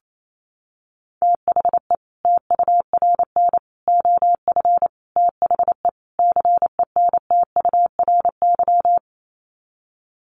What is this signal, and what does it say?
Morse code, heard as a keyed tone, about 28 wpm, 700 Hz. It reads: THE TURN OF THE CENTURY
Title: Sets of 5 Words 28wpm